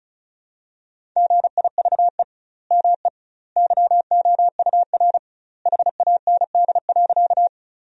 Morse code, heard as a keyed tone, about 36 wpm, 700 Hz. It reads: GIVE ME YOUR HAND.
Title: 2x Word-Spacing - Sentences from Top 200 Words - 35wpm